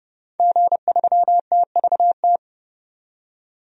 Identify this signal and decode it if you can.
Morse code, heard as a keyed tone, about 32 wpm, 700 Hz. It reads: G3TVT
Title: DXCC-100 Call Signs - 5 Characters - 30wpm